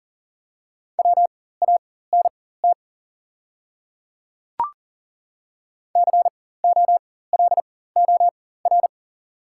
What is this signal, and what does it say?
Morse code, heard as a keyed tone, about 40 wpm, spaced as for 19 wpm, 700 Hz. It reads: WANT COLOR
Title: 4x Character Spacing - Top 500 Words - 40wpm